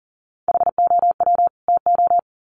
Morse code, heard as a keyed tone, about 40 wpm, 700 Hz. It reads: HOW TO
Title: Sets of 2 Words 40wpm